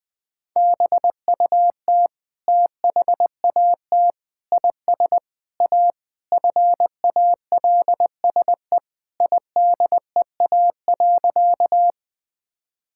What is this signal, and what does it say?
Morse code, heard as a keyed tone, about 20 wpm, 700 Hz. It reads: BUT THAT IS A FALSE IDEA.
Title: Sentences from Top 1600 Words 20wpm